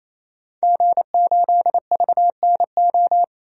Morse code, heard as a keyed tone, about 28 wpm, 700 Hz. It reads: G8VNO
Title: DXCC-100 Call Signs - 5 Characters - 28wpm